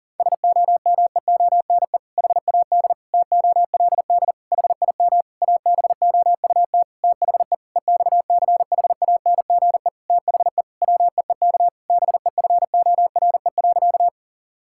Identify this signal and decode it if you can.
Morse code, heard as a keyed tone, about 40 wpm, 690 Hz. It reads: SOMEONE HAD TOLD HIM ABOUT THE EXCHANGE THE WEEK BEFORE.
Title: Sentences from Top 1000 Words - Review Part 6 - 40wpm